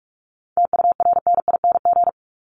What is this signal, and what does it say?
Morse code, heard as a keyed tone, about 44 wpm, 700 Hz. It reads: TURNING
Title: ING Suffix - Top 100 Words with Base Words 45wpm